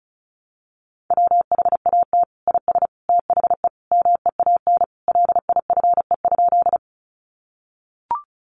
Morse code, heard as a keyed tone, about 34 wpm, 700 Hz. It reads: WHAT IS THE MEAN LIFE?
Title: Sentences from Top 900 Words 35wpm